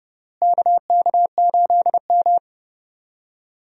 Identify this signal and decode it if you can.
Morse code, heard as a keyed tone, about 30 wpm, 700 Hz. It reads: KK8M
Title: USA Call Signs - 4 Characters - Repeated 3x - 30wpm